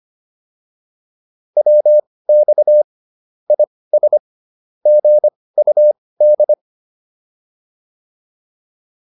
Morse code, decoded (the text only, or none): WX IS GUD